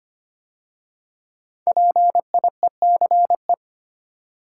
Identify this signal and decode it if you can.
Morse code, heard as a keyed tone, about 25 wpm, 700 Hz. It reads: PIECE
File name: 5 Letter Words - First 3 Letters Given - 25wpm